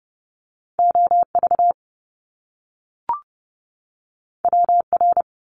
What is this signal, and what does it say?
Morse code, heard as a keyed tone, about 30 wpm, 700 Hz. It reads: OV WR
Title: Top Two Letter Combinations in Words 30wpm